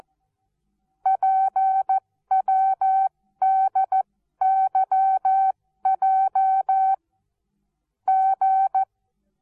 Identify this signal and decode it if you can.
Morse code, heard as a keyed tone, about 14 wpm, spaced as for 12 wpm, 770 Hz. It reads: PWDYJ G
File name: Onlinelehrgang / morsen / DJ4UF Morsen19